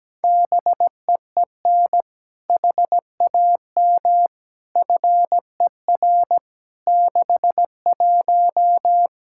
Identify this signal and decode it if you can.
Morse code, heard as a keyed tone, about 17 wpm, 700 Hz. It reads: BEEN HAM FER 61
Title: K1USN SST QSO Practice 17wpm